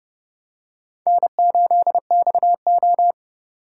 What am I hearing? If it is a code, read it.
Morse code, heard as a keyed tone, about 29 wpm, 700 Hz. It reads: N8XO